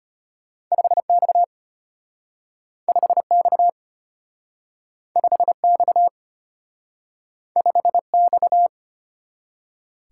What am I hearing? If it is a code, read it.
Morse code, heard as a keyed tone, about 33 wpm, 700 Hz. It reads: 5X 5X 5X 5X